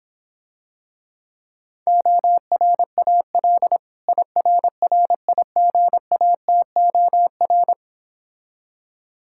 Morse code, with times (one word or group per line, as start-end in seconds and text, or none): 1.9-3.8 ORAL
4.1-7.7 IRRIGATOR